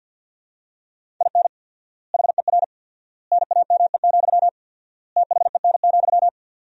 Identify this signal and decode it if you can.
Morse code, heard as a keyed tone, about 52 wpm, 700 Hz. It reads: IN HER NAME, THEN,
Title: 4x Word-Spacing - Sentences from Top 500 Words - 50wpm